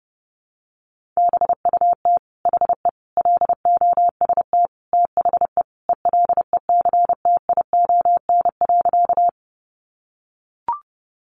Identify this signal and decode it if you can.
Morse code, heard as a keyed tone, about 30 wpm, 700 Hz. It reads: BUT HE LOST THE ELECTION.
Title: Sentences from Top 1500 Words 30wpm